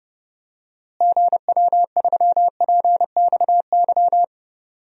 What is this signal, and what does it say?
Morse code, heard as a keyed tone, about 29 wpm, 700 Hz. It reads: GW3PXY
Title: DXCC-100 Call Signs - 6 Characters - Repeated 3x - 30wpm